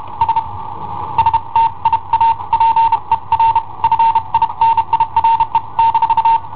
Morse code, heard as a keyed tone, about 31 wpm, 910 Hz. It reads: S STIA PER FINIRE -
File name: i7ohp / cassetta i7ohp B